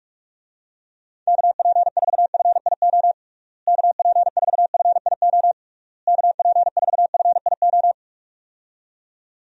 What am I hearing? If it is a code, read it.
Morse code, heard as a keyed tone, about 46 wpm, 700 Hz. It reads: KP4FIO KP4FIO KP4FIO
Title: USA Call Signs - 6 Characters - Repeated 3x - 45wpm